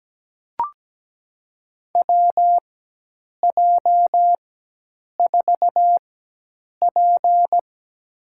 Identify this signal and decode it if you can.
Morse code, heard as a keyed tone, about 17 wpm, spaced as for 8 wpm, 700 Hz. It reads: WJ4P
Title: USA Call Signs - 4 Characters - 4x Character Spacing - 17wpm